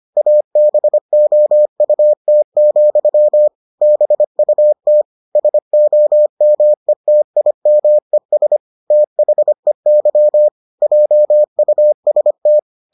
Morse code, heard as a keyed tone, about 25 wpm, 600 Hz. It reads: ABOUT, BUT SOMETIMES THEY JUST